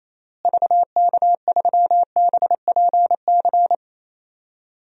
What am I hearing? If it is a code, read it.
Morse code, heard as a keyed tone, about 27 wpm, 700 Hz. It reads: VK3BPC
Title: DXCC-100 Call Signs - 6 Characters - Repeated 3x - 28wpm